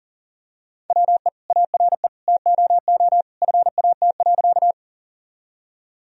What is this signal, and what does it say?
Morse code, heard as a keyed tone, about 40 wpm, 700 Hz. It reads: WE ARE TOO FAT.
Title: Sentences from Top 1000 Words - Review Part 4 - 40wpm